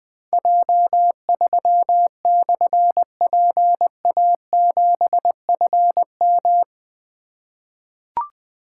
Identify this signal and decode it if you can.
Morse code, heard as a keyed tone, about 21 wpm, 700 Hz. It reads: J3/PA7FM
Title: Difficult Call Signs 20wpm